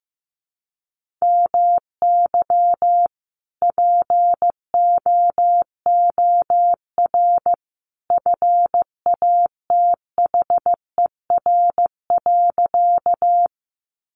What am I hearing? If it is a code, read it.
Morse code, heard as a keyed tone, about 15 wpm, 700 Hz. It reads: MY POOR FATHER.
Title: Sentences from Top 1000 Words 15wpm